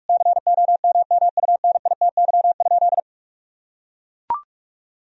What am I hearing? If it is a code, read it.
Morse code, heard as a keyed tone, about 41 wpm, 700 Hz. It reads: COMMUNITY?
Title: Sentences from Top 1000 Words - Review Part 4 - 45wpm